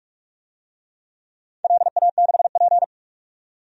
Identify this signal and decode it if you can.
Morse code, heard as a keyed tone, about 46 wpm, 700 Hz. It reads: LA6P
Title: DXCC-100 Call Signs - 4 Characters - 45wpm